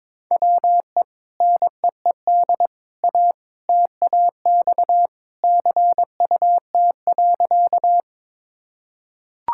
Morse code, heard as a keyed tone, about 22 wpm, 700 Hz. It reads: WE NEED A TAX CUT.
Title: Sentences from Top 1000 Words - Review Part 5 - 22wpm